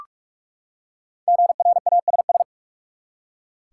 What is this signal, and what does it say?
Morse code, heard as a keyed tone, about 44 wpm, 700 Hz. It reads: GRASS
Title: Five Letter Words - 45wpm